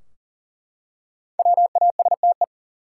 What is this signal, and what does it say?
Morse code, heard as a keyed tone, about 40 wpm, 700 Hz. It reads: WASTE
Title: Five Letter Words - 40wpm